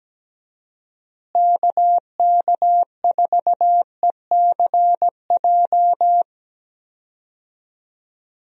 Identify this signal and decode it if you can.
Morse code, heard as a keyed tone, about 17 wpm, 700 Hz. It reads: KK4ECJ
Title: USA Call Signs - 6 Characters - Repeated 3x - 17wpm